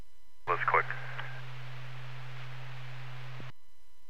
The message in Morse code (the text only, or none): none